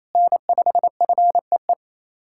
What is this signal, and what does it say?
Morse code, heard as a keyed tone, about 27 wpm, 700 Hz. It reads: N5FEE
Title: USA Callsign DE Callsign 28wpm